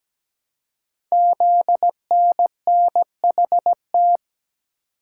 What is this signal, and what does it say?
Morse code, heard as a keyed tone, about 17 wpm, 700 Hz. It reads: ZNNHT
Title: Code Groups - 5 Letters - 17wpm